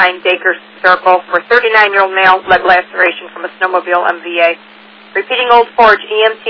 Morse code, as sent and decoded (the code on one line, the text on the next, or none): none